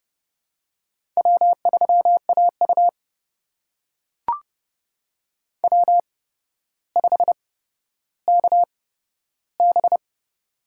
Morse code, heard as a keyed tone, about 30 wpm, spaced as for 13 wpm, 700 Hz. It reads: W3AU W5KB
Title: USA Call Signs - 4 Characters - 8x Character Spacing - 30wpm